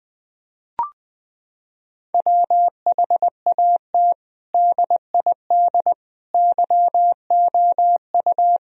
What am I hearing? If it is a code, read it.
Morse code, heard as a keyed tone, about 20 wpm, 700 Hz. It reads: WHAT DID YOU
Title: Sentences from Top 1000 Words - Review Part 4 - 20wpm